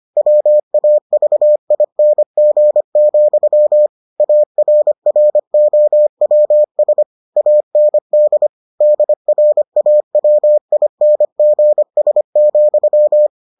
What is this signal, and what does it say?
Morse code, heard as a keyed tone, about 25 wpm, 600 Hz. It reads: WAVING, ARROWS AND DRAWINGS,